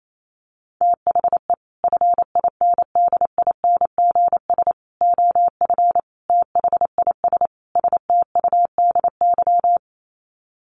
Morse code, heard as a keyed tone, about 28 wpm, 700 Hz. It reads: THE FINDINGS OF THIS STUDY
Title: Sets of 5 Words Encore2 28wpm